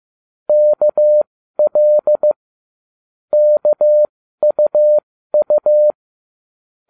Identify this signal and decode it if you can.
Morse code, heard as a keyed tone, about 15 wpm, spaced as for 12 wpm, 600 Hz. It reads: KL KUU